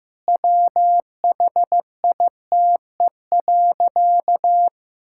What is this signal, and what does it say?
Morse code, heard as a keyed tone, about 15 wpm, 700 Hz. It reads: WHITE.
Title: Sentences from Top 1000 Words - Review Part 6 - 15wpm